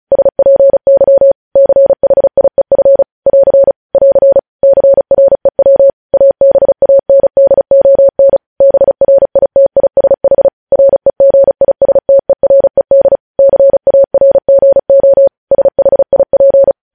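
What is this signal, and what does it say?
Morse code, heard as a keyed tone, about 36 wpm, 550 Hz. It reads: SPY CHIEF <AR> <AR> CREW ABANDON BRITISH REGISTERED CARGO SHIP
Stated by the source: Morse code news headlines sent at 35.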